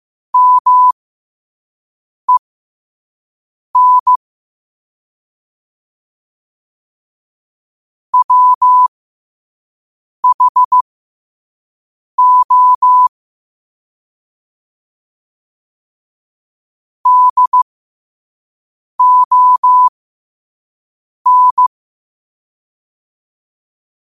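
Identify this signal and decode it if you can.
Morse code, heard as a keyed tone, about 15 wpm, spaced as for 5 wpm, 1000 Hz. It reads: MEN WHO DON
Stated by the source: Morse code sent at 5 words per minute.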